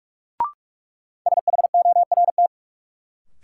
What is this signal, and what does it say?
Morse code, heard as a keyed tone, about 41 wpm, 700 Hz. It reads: SHORT
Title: Sets of 5 Words Encore - New Word Drill - 45wpm